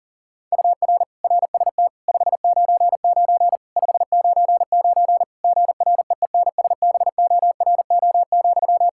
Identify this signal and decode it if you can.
Morse code, heard as a keyed tone, about 40 wpm, 700 Hz. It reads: UR RST 599 599 GREENSBORO,
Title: K1USN SST QSO Practice 40wpm